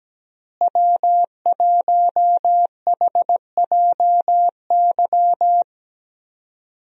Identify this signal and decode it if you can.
Morse code, heard as a keyed tone, about 17 wpm, 700 Hz. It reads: W1HJY
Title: USA Call Signs - 5 Characters - Repeated 2x - 17wpm